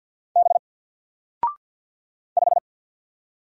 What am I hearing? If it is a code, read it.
Morse code, heard as a keyed tone, about 48 wpm, 700 Hz. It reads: B 5
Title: Single Letter-Number 50wpm